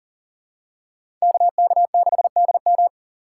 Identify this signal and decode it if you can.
Morse code, heard as a keyed tone, about 40 wpm, 700 Hz. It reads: KK6DM
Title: USA Call Signs - 5 Characters - 3x Character Spacing - 40wpm